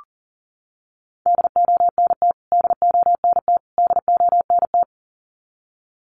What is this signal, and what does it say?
Morse code, heard as a keyed tone, about 40 wpm, 700 Hz. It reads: DONT DONT DONT
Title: Top 500 Words Repeated 40wpm